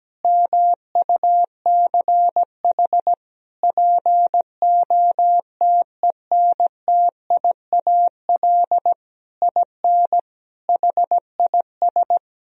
Morse code, heard as a keyed tone, about 17 wpm, 700 Hz. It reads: MUCH POTENTIAL IN HIS
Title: Sentences from Top 1000 Words - Review Part 4 - 17wpm